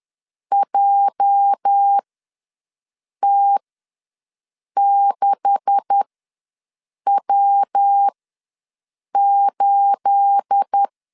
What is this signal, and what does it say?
Morse code, heard as a keyed tone, about 10 wpm, spaced as for 6 wpm, 800 Hz. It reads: JT6W8
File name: LEZIONI / 9 286 ws 10wpm 4032